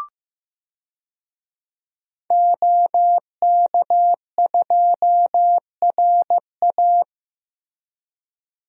Morse code, heard as a keyed tone, about 15 wpm, 700 Hz. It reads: OK2RA
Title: DXCC-100 Call Signs - 5 Characters - Repeated 2x - 15wpm